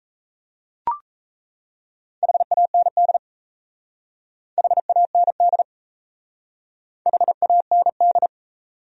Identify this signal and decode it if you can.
Morse code, heard as a keyed tone, about 34 wpm, 700 Hz. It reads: HAND HAND HAND
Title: Speed-Racing Top 200 Words 28wpm